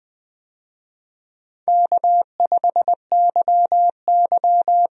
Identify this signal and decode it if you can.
Morse code, heard as a keyed tone, about 20 wpm, 700 Hz. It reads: K5YY
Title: USA Call Signs - 4 Characters - 20wpm